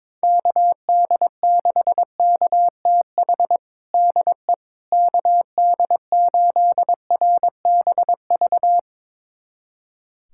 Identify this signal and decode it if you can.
Morse code, heard as a keyed tone, about 22 wpm, 700 Hz. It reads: KD6KTH DE KD8RBV
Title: USA Callsign DE Callsign - 6 Characters - 22wpm